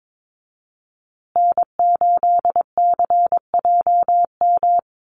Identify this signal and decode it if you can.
Morse code, heard as a keyed tone, about 22 wpm, 700 Hz. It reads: N8CJM
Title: USA Call Signs Repeated 3x 22wpm